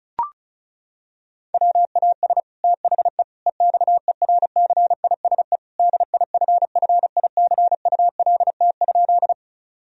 Morse code, heard as a keyed tone, about 36 wpm, 700 Hz. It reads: WAS THE EXERCISE DIFFICULT?
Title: Sentences from Top 1000 Words - Review Part 4 - 35wpm